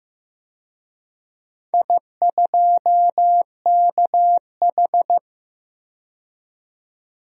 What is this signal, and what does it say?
Morse code, heard as a keyed tone, about 15 wpm, 700 Hz. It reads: I2KH
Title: DXCC-100 Call Signs - 4 Characters - 15wpm